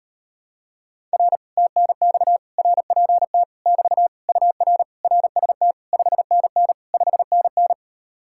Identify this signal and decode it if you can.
Morse code, heard as a keyed tone, about 38 wpm, 700 Hz. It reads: R TNX RPT <BT> UR RST 5NN 5NN